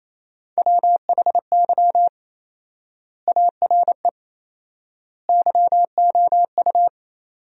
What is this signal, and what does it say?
Morse code, heard as a keyed tone, about 28 wpm, 700 Hz. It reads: WHY ARE YOU
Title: 4x Word-Spacing - Sentences from Top 200 Words - 28wpm